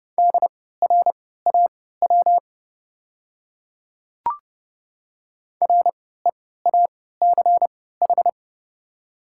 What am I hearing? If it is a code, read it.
Morse code, heard as a keyed tone, about 30 wpm, spaced as for 17 wpm, 690 Hz. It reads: DRAW REACH
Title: 3x Character Spacing - Top 500 Words - 30wpm